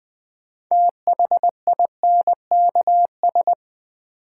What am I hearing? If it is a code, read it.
Morse code, heard as a keyed tone, about 20 wpm, 700 Hz. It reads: THINKS
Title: Sentences from Top 1000 Words - Review Part 6 - New Word Drill - 20wpm